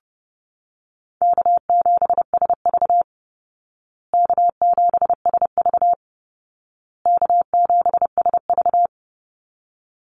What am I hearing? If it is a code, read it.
Morse code, heard as a keyed tone, about 30 wpm, 700 Hz. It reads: K7SV K7SV K7SV
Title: Easy Call Signs Repeated 3x - 30wpm